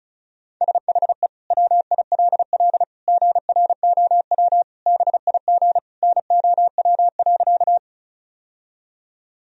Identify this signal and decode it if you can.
Morse code, heard as a keyed tone, about 36 wpm, 700 Hz. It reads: SHE WILL GROW BIG NOW.
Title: Sentences from Top 1000 Words - Review Part 1 - 35wpm